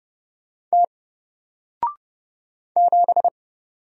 Morse code, heard as a keyed tone, about 30 wpm, 700 Hz. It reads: T 7
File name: Single-Letter-Number-30wpm